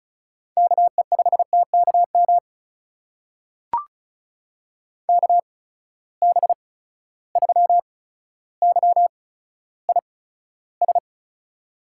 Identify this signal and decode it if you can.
Morse code, heard as a keyed tone, about 34 wpm, spaced as for 15 wpm, 700 Hz. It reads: KE5TKM KB3YIS
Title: USA Call Signs - 6 Characters - 8x Character Spacing - 35wpm